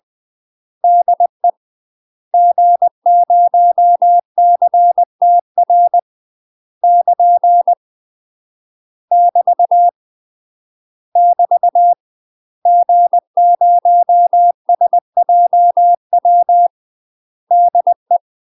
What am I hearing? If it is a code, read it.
Morse code, heard as a keyed tone, about 20 wpm, 700 Hz. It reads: DE G0CTR <KN> <BT> <BT> G0SJW DE
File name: UK MASTER QSOs TRAINING / 20WPM / UK MASTER QSOs 0030